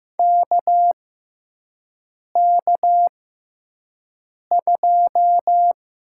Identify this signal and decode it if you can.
Morse code, heard as a keyed tone, about 15 wpm, spaced as for 5 wpm, 700 Hz. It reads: KK2
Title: USA Call Signs - 5 Characters - 6x Character Spacing - 15wpm